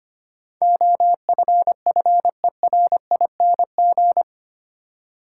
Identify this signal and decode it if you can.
Morse code, heard as a keyed tone, about 25 wpm, 690 Hz. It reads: OFFERING
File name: 8 Letter Words - First 3 Letters Given - 25wpm